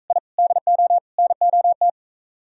Morse code, heard as a keyed tone, about 45 wpm, 700 Hz. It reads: I DO NOT